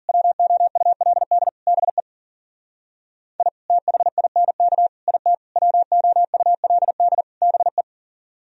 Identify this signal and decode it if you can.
Morse code, heard as a keyed tone, about 50 wpm, 700 Hz. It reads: WOULD BE I THINK IT WOULD BE
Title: Speed-Racing - Sets of 5 Words 40wpm